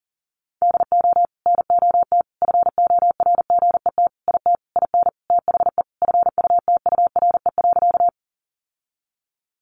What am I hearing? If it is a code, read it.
Morse code, heard as a keyed tone, about 40 wpm, 700 Hz. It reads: DO NOT FORGET IT IN THE FUTURE.
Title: Sentences from Top 1200 Words 40wpm